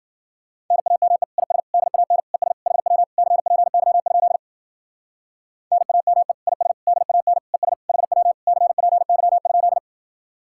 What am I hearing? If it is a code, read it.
Morse code, heard as a keyed tone, about 51 wpm, 700 Hz. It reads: NAME IS DAN ES HW CPY? NAME IS DAN ES HW CPY?